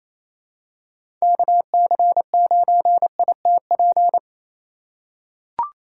Morse code, heard as a keyed tone, about 28 wpm, 700 Hz. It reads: KC9ITP
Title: USA Call Signs - 6 Characters - 3x Character Spacing - 28wpm